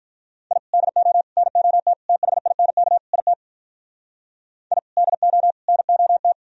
Morse code, heard as a keyed tone, about 46 wpm, 700 Hz. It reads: I DO NOT THINK IT I DO NOT